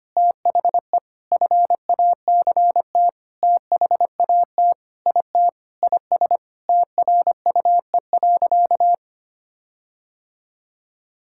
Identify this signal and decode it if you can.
Morse code, heard as a keyed tone, about 25 wpm, 700 Hz. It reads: THE FACT THAT IT IS TRUE.
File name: practice / 25wpm / Sentences with 4-Word Phrases - 25wpm